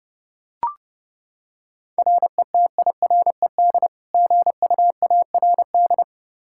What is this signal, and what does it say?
Morse code, heard as a keyed tone, about 28 wpm, 700 Hz. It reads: RETIRED GUARD
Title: K1USN SST QSO Practice 30wpm